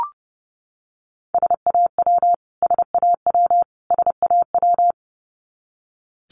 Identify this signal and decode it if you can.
Morse code, heard as a keyed tone, about 30 wpm, 700 Hz. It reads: SAW SAW SAW
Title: Top 500 Words Repeated 30wpm